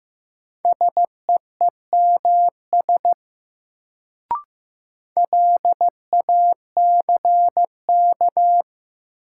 Morse code, heard as a keyed tone, about 15 wpm, 700 Hz. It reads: SEEMS LACK
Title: Sets of 3 Words - New Word Drill - 15wpm